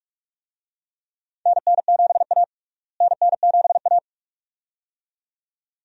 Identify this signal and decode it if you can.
Morse code, heard as a keyed tone, about 42 wpm, 700 Hz. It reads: NN7A NN7A
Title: USA Call Signs - 4 Characters - Repeated 2x - 45wpm